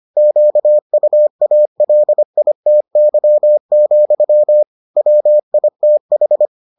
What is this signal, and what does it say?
Morse code, heard as a keyed tone, about 25 wpm, 600 Hz. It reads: QUALITY, WITH